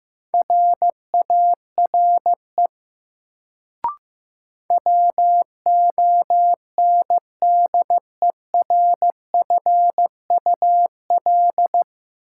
Morse code, heard as a keyed tone, about 16 wpm, 700 Hz. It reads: RARE WONDERFUL
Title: Sentences from Top 1100 Words - New Word Drill Alt - 15wpm